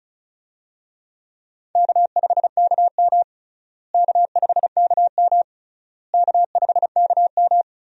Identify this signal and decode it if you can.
Morse code, heard as a keyed tone, about 36 wpm, 700 Hz. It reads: K5KM K5KM K5KM
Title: USA Call Signs - 4 Characters - Repeated 3x - 35wpm